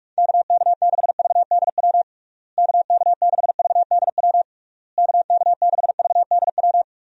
Morse code, heard as a keyed tone, about 46 wpm, 700 Hz. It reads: KK6VDW KK6VDW KK6VDW
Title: USA Call Signs - 6 Characters - Repeated 3x - 45wpm